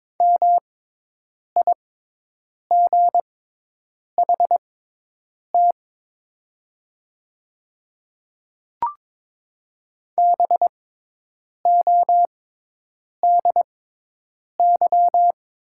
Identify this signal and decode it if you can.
Morse code, heard as a keyed tone, about 22 wpm, spaced as for 8 wpm, 700 Hz. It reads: MIGHT BODY
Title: 6x Character Spacing - Top 500 Words - 22wpm